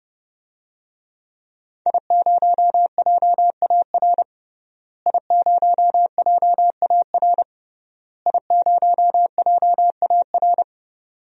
Morse code, heard as a keyed tone, about 30 wpm, 700 Hz. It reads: I0JAR I0JAR I0JAR
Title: DXCC-100 Call Signs - 5 Characters - Repeated 3x - 30wpm